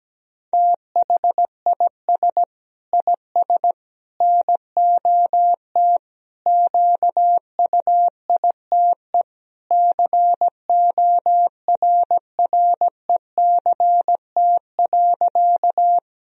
Morse code, heard as a keyed tone, about 17 wpm, 700 Hz. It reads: THIS IS NOT QUITE CORRECT.
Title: Sentences from Top 1000 Words - Review Part 2 - 17wpm